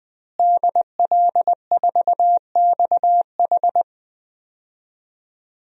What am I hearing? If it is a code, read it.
Morse code, heard as a keyed tone, about 21 wpm, 700 Hz. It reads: DL4XH
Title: DXCC-100 Call Signs - 5 Characters - Repeated 3x - 20wpm